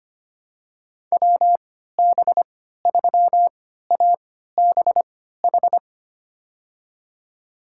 Morse code, heard as a keyed tone, about 25 wpm, spaced as for 14 wpm, 700 Hz. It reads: WB3ABH
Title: USA Call Signs - 6 Characters - 3x Character Spacing - 25wpm